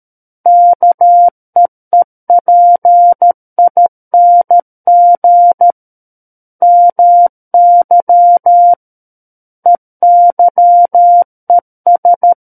Morse code, heard as a keyed tone, about 13 wpm, 700 Hz. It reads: KEEPING MY EYES
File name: Book One / 04-Chapter 4 - 13 WPM